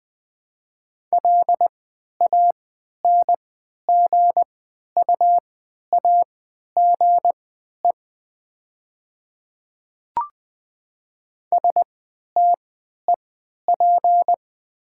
Morse code, heard as a keyed tone, about 20 wpm, spaced as for 11 wpm, 700 Hz. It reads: LANGUAGE STEP
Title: 3x Character Spacing - Top 500 Words - 20wpm